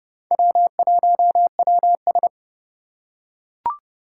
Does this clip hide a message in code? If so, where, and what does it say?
Yes: Morse code from 0.3 to 2.3 s, heard as a keyed tone, about 28 wpm, 700 Hz. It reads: W1WS